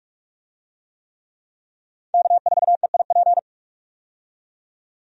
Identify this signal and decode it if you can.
Morse code, heard as a keyed tone, about 43 wpm, 700 Hz. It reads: K4EIP
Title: USA Call Signs - 5 Characters - 45wpm